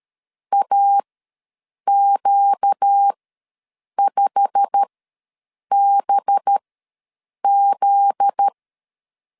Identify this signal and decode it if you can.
Morse code, heard as a keyed tone, about 13 wpm, spaced as for 7 wpm, 800 Hz. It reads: AQ5BZ
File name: LEZIONI / 9 28 ws 12wpm 3354